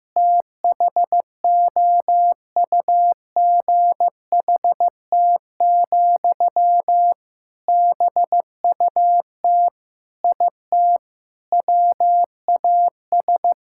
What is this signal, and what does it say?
Morse code, heard as a keyed tone, about 15 wpm, 700 Hz. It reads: THOUGHT, BUT IT WAS
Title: Sentences from Top 1000 Words - Review Part 6 - 15wpm